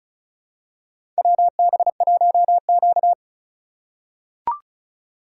Morse code, heard as a keyed tone, about 36 wpm, 690 Hz. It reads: WB1Q
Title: USA Call Signs - 4 Characters - 3x Character Spacing - 35wpm